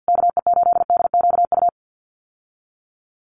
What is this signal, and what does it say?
Morse code, heard as a keyed tone, about 50 wpm, 700 Hz. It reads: KE8DQU
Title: USA Call Signs Repeated 3x 50wpm